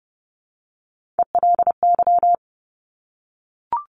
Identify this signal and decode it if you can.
Morse code, heard as a keyed tone, about 30 wpm, 700 Hz. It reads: ELY
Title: Top Three Letter Combinations in Words 30wpm